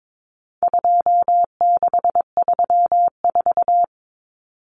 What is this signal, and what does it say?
Morse code, heard as a keyed tone, about 22 wpm, 700 Hz. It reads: 2634
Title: Numbers - 4-Digit 22wpm